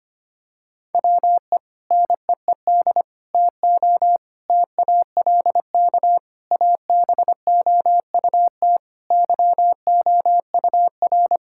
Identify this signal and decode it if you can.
Morse code, heard as a keyed tone, about 25 wpm, 700 Hz. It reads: WE NEED TO TALK ABOUT YOUR
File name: practice / 25wpm / Sentences with 3-Word Phrases - 25wpm